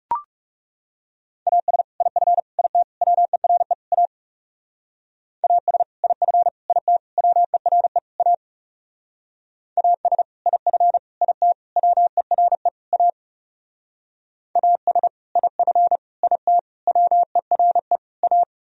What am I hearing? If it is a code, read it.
Morse code, heard as a keyed tone, about 35 wpm, 700 Hz. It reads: AS IF IT WERE A AS IF IT WERE A AS IF IT WERE A AS IF IT WERE A
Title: Speed-Racing - Sets of 5 Words 30wpm